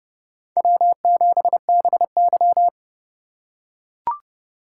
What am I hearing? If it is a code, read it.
Morse code, heard as a keyed tone, about 28 wpm, 690 Hz. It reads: W7BY